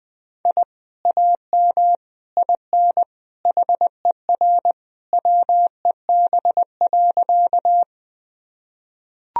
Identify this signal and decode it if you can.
Morse code, heard as a keyed tone, about 20 wpm, 700 Hz. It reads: I AM IN HER WEB.
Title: Sentences from Top 1000 Words - Review Part 4 - 20wpm